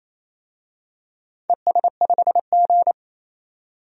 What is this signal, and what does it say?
Morse code, heard as a keyed tone, about 29 wpm, 700 Hz. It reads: ES5G
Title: DXCC-100 Call Signs - 4 Characters - 28wpm